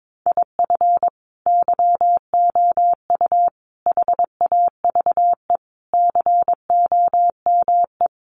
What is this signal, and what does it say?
Morse code, heard as a keyed tone, about 22 wpm, 700 Hz. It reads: IF YOU HAVE COME
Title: Sentences from Top 800 Words 22wpm